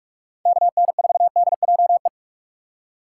Morse code, heard as a keyed tone, about 45 wpm, 700 Hz. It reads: KN4DJE
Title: USA Call Signs - 6 Characters - 4x Character Spacing - 45wpm